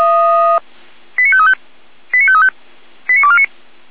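Five-tone selective call (ZVEI), heard as quick runs of five tones, tones of about 70 ms. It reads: 89535 89535 89249
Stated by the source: ZVEI five-tone call received by SDR.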